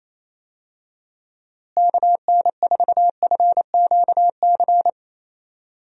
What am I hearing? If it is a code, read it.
Morse code, heard as a keyed tone, about 28 wpm, 700 Hz. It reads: KN4FQC
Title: USA Call Signs - 6 Characters - Repeated 2x - 28wpm